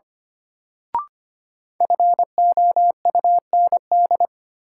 Morse code, heard as a keyed tone, about 25 wpm, 700 Hz. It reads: FOUND
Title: Sentences from Top 300 Words - New Word Drill - 25wpm